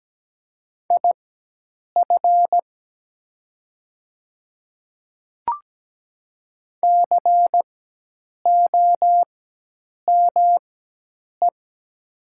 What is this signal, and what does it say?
Morse code, heard as a keyed tone, about 17 wpm, spaced as for 8 wpm, 700 Hz. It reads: IF COME
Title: 4x Character Spacing - Top 100 Words - 17wpm